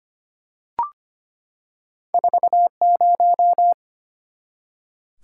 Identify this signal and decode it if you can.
Morse code, heard as a keyed tone, about 25 wpm, 690 Hz. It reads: 40
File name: Two Letter-Number Combinations - 25wpm